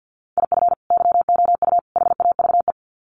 Morse code, heard as a keyed tone, about 48 wpm, 700 Hz. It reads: IF YOU HAVE
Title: Sets of 3 Words 50wpm